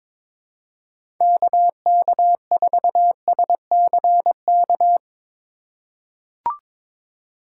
Morse code, heard as a keyed tone, about 22 wpm, 700 Hz. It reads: KK4SCK
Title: USA Call Signs - 3x Character Spacing - 22wpm